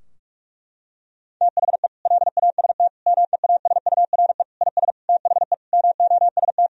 Morse code, heard as a keyed tone, about 48 wpm, 700 Hz. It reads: THE LAST MEASURE IS THE MOST
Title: Sentences from Top 1000 Words - Review Part 2 - 45wpm